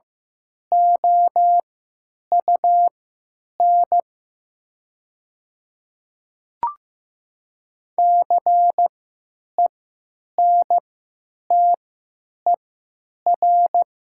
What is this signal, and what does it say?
Morse code, heard as a keyed tone, about 15 wpm, spaced as for 9 wpm, 700 Hz. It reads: OUN CENTER
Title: 3x Character Spacing - Top 500 Words - 15wpm